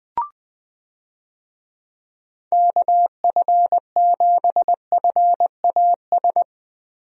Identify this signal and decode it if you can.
Morse code, heard as a keyed tone, about 20 wpm, 700 Hz. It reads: KF7FAS
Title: USA Call Signs - 6 Characters - Repeated 2x - 20wpm